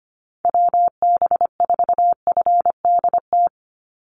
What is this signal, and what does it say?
Morse code, heard as a keyed tone, about 25 wpm, 700 Hz. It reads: WB4FDT
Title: Call Signs Repeated 2x - 25wpm